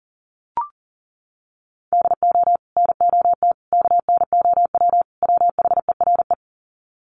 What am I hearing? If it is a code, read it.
Morse code, heard as a keyed tone, about 40 wpm, 700 Hz. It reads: DO NOT KNOW WHERE
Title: Sets of 4 Words 40wpm